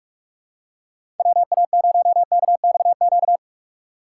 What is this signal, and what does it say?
Morse code, heard as a keyed tone, about 46 wpm, 690 Hz. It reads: WA0KXQ
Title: USA Call Signs - 3x Character Spacing - 45wpm